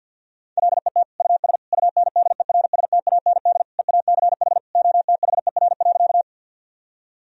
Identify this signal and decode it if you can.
Morse code, heard as a keyed tone, about 50 wpm, 700 Hz. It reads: LET US UNDERSTAND EACH OTHER.
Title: Sentences from Top 1000 Words - Review Part 1 - 50wpm